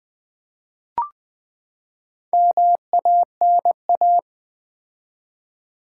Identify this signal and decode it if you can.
Morse code, heard as a keyed tone, about 20 wpm, 700 Hz. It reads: MANA
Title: Build A Word - 4 Letters At A Time - 10-Letter Words - 20wpm